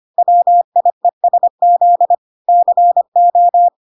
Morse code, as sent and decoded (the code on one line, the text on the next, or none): .-- .. ... --.. / -.-. ---
WIESZ CO